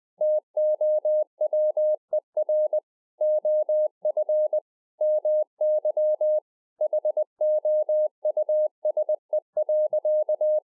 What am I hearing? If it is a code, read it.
Morse code, heard as a keyed tone, about 20 wpm, 610 Hz. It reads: TOWER OF MY HOUSE.